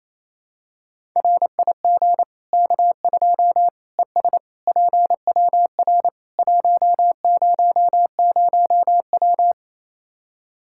Morse code, heard as a keyed tone, about 28 wpm, 700 Hz. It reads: RIG K2 ES PWR 100W
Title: Practice QSO - Exchange 2 - Info - 28wpm